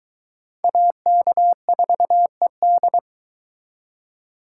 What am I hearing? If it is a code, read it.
Morse code, heard as a keyed tone, about 21 wpm, 700 Hz. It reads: AK4ED